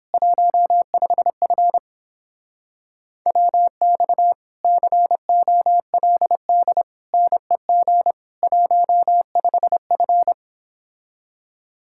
Morse code, heard as a keyed tone, about 29 wpm, 700 Hz. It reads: 15F WX COLD NEG 15F